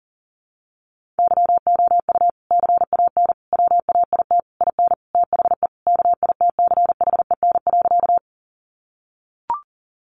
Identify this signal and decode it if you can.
Morse code, heard as a keyed tone, about 40 wpm, 700 Hz. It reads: YOU CAN WAIT IN THE KITCHEN.
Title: Sentences from Top 1400 Words 40wpm